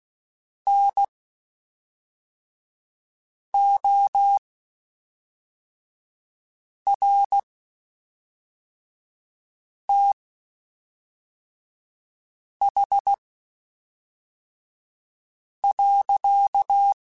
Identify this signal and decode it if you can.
Morse code, heard as a keyed tone, about 16 wpm, spaced as for 3 wpm, 800 Hz. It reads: NORTH.
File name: Spaced Characters / Spaced Characters-mp3s / Weather4 24 sp @5